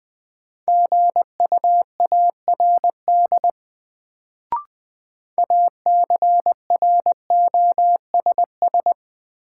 Morse code, heard as a keyed tone, about 20 wpm, 700 Hz. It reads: GUARD ACROSS
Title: Sentences from Top 700 Words - New Word Drill - 20wpm